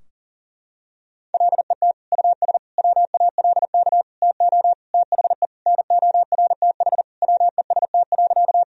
Morse code, heard as a keyed tone, about 40 wpm, 700 Hz. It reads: LET US WALK TO THE NORTH WEST.
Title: Sentences from Top 1000 Words - Review Part 2 - 40wpm